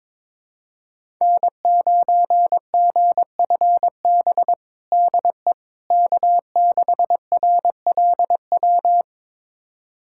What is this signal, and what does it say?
Morse code, heard as a keyed tone, about 22 wpm, 700 Hz. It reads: N9GFB DE K6RLW
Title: USA Callsign DE Callsign - 5 Characters - 22wpm